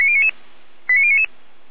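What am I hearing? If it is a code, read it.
Five-tone selective call (ZVEI), heard as quick runs of five tones, tones of about 70 ms. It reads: 89099
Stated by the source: ZVEI five-tone call received by SDR.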